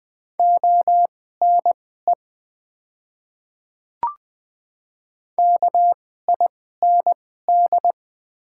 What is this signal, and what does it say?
Morse code, heard as a keyed tone, about 20 wpm, spaced as for 14 wpm, 700 Hz. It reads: ONE KIND
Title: 2x Character Spacing - Top 200 Words - 20wpm